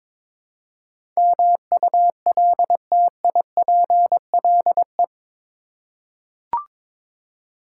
Morse code, heard as a keyed tone, about 22 wpm, 700 Hz. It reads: MULTIPLE
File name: practice / 22wpm / Eight Letter Words - 22wpm